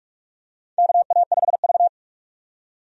Morse code, heard as a keyed tone, about 46 wpm, 700 Hz. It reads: KA5V
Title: USA Call Signs - 4 Characters - 45wpm